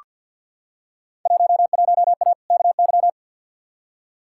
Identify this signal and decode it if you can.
Morse code, heard as a keyed tone, about 47 wpm, 700 Hz. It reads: 11A KY
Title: Field Day Exchanges Encore - 50wpm